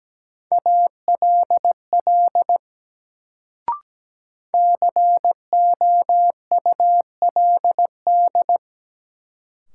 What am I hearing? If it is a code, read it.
Morse code, heard as a keyed tone, about 17 wpm, 700 Hz. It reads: ALL COULD
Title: Sentences from Top 100 Words - New Word Drill - 17wpm